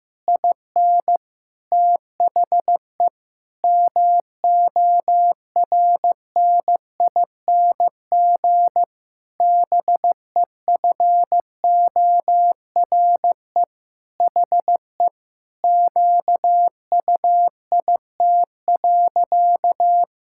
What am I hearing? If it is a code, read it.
Morse code, heard as a keyed tone, about 15 wpm, 700 Hz. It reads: IN THE MORNING BEFORE HE QUIT.
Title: Pangram Challenge v2 - 15wpm